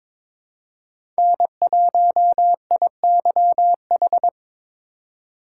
Morse code, heard as a keyed tone, about 22 wpm, 700 Hz. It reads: N1IYH
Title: USA Call Signs - 5 Characters - 3x Character Spacing - 22wpm